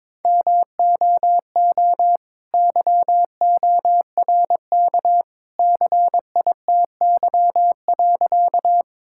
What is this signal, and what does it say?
Morse code, heard as a keyed tone, about 22 wpm, 700 Hz. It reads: MOO YORK CITY.
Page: morse-code-ninja-practice-22wpm